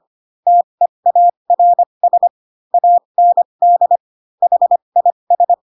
Morse code, heard as a keyed tone, about 25 wpm, spaced as for 22 wpm, 700 Hz. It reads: TEARS AND HIS